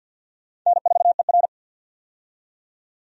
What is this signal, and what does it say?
Morse code, heard as a keyed tone, about 48 wpm, 700 Hz. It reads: N4ER
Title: USA Call Signs - 4 Characters - 50wpm